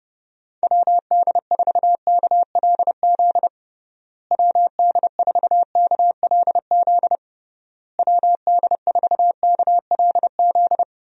Morse code, heard as a keyed tone, about 30 wpm, 700 Hz. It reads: WD4KLZ WD4KLZ WD4KLZ
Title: USA Call Signs - 6 Characters - Repeated 3x - 30wpm